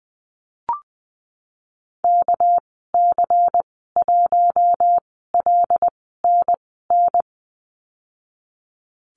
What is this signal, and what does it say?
Morse code, heard as a keyed tone, about 20 wpm, spaced as for 15 wpm, 700 Hz. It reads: KC1LNN
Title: 207 Lesson 33g - Letter j - Call Signs 20wpm